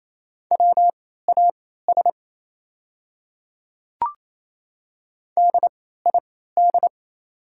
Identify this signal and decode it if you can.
Morse code, heard as a keyed tone, about 28 wpm, spaced as for 16 wpm, 690 Hz. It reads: WAS DID
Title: 3x Character Spacing - Top 100 Words - 28wpm